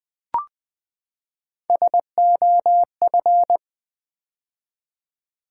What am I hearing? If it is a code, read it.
Morse code, heard as a keyed tone, about 20 wpm, 700 Hz. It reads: SOF
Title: Build A Word - 3 Letters At A Time - 8-Letter Words - 20wpm